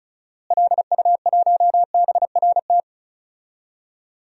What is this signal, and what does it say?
Morse code, heard as a keyed tone, about 36 wpm, 700 Hz. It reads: LU1BRT
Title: DXCC-100 Call Signs - 6 Characters - Repeated 3x - 35wpm